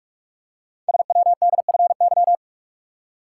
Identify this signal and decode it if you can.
Morse code, heard as a keyed tone, about 44 wpm, 700 Hz. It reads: SWDFY
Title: Code Groups - 5 Letters - 45wpm